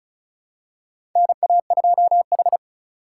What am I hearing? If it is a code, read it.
Morse code, heard as a keyed tone, about 36 wpm, 700 Hz. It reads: NA2H
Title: USA Call Signs - 4 Characters - 2x Character Spacing - 35wpm